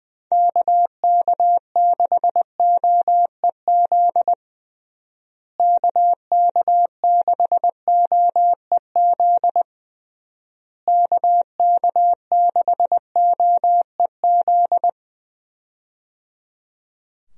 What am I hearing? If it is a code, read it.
Morse code, heard as a keyed tone, about 20 wpm, 700 Hz. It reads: KK6OEZ KK6OEZ KK6OEZ
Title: USA Call Signs - 6 Characters - Repeated 3x - 20wpm